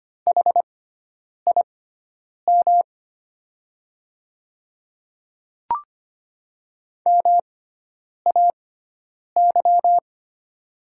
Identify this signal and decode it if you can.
Morse code, heard as a keyed tone, about 25 wpm, spaced as for 9 wpm, 700 Hz. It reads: HIM MAY
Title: 6x Character Spacing - Top 100 Words - 25wpm